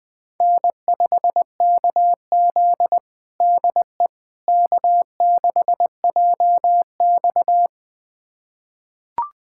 Morse code, heard as a keyed tone, about 20 wpm, 700 Hz. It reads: N5KZ DE K6JX
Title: USA Callsign DE Callsign - 4 Characters - 20wpm